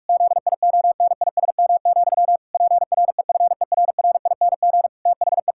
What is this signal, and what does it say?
Morse code, heard as a keyed tone, about 41 wpm, 700 Hz. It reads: ZIONISM, PREFERRING THE
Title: Pangram Challenge v2 - 45wpm